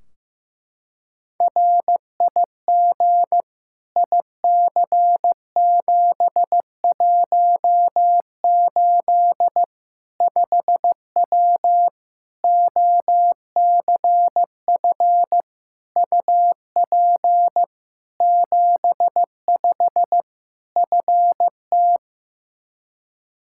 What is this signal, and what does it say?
Morse code, heard as a keyed tone, about 15 wpm, 700 Hz. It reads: RIG IC718 5W OCF UP 75 FT
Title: K1USN SST QSO Practice 15wpm